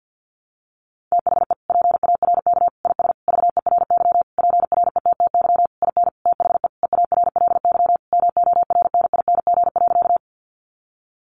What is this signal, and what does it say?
Morse code, heard as a keyed tone, about 50 wpm, 700 Hz. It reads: THE PARK IS VERY PRETTY IN THE EARLY MORNING.
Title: Sentences from Top 1300 Words 50wpm